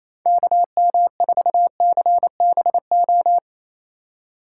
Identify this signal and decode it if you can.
Morse code, heard as a keyed tone, about 36 wpm, 700 Hz. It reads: KM4CBO